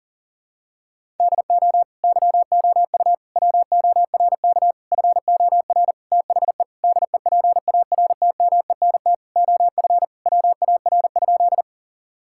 Morse code, heard as a keyed tone, about 40 wpm, 700 Hz. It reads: DO YOU WORK FOR THE DEPARTMENT OF WAR?
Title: Sentences from Top 1000 Words - Review Part 5 - 40wpm